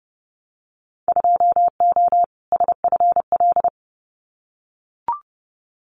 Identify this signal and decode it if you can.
Morse code, heard as a keyed tone, about 30 wpm, 700 Hz. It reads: 2O SFL
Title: Winter Field Day Exchanges 30wpm